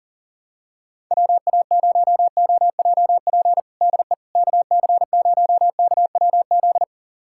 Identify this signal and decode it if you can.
Morse code, heard as a keyed tone, about 37 wpm, 700 Hz. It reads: WA0OJP DE KC0KWZ
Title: USA Callsign DE Callsign - 6 Characters - 40wpm